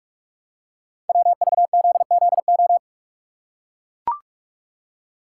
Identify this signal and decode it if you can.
Morse code, heard as a keyed tone, about 44 wpm, 690 Hz. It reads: WVZZO
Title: Code Groups - 5 Letters - 45wpm